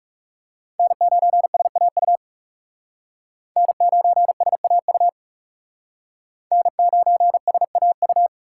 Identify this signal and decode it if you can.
Morse code, heard as a keyed tone, about 37 wpm, 700 Hz. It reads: N9SAU N9SAU N9SAU
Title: Speed Racing - USA Call Signs 30wpm